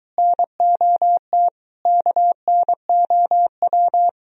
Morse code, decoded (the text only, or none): NOT KNOW